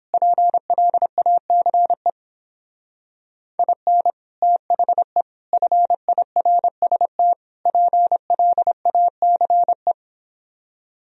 Morse code, heard as a keyed tone, about 29 wpm, 700 Hz. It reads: PLACE IN THE FIRST PLACE